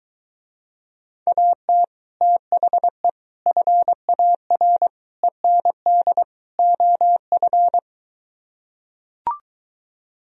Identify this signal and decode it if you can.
Morse code, heard as a keyed tone, about 24 wpm, 700 Hz. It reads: AT THE FAR END OF